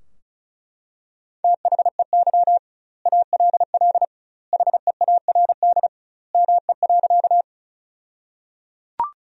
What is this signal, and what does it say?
Morse code, heard as a keyed tone, about 34 wpm, 700 Hz. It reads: THEY ALL HEARD ME.
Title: 2x Word-Spacing - Sentences from Top 400 Words - 35wpm